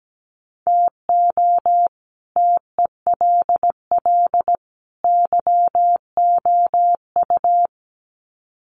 Morse code, heard as a keyed tone, about 17 wpm, 700 Hz. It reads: TO TELL YOU
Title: Sets of 3 Words 17wpm